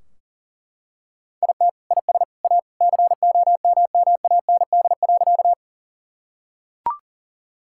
Morse code, heard as a keyed tone, about 40 wpm, 700 Hz. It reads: IT IS A COMMAND.
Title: Sentences from Top 1600 Words 40wpm